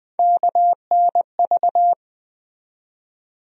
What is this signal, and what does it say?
Morse code, heard as a keyed tone, about 21 wpm, 700 Hz. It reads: KNV
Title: USA Call Sign Postfixes - 3 Letters 20wpm